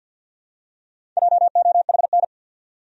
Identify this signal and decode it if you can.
Morse code, heard as a keyed tone, about 51 wpm, 700 Hz. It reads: JOHN
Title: USA Common Names - Max 4 Letters - 50wpm